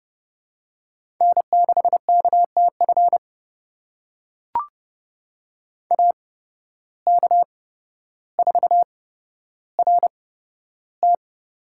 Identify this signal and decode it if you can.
Morse code, heard as a keyed tone, about 30 wpm, spaced as for 13 wpm, 700 Hz. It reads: N6KTF AK4RT
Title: USA Call Signs - 5 Characters - 8x Character Spacing - 30wpm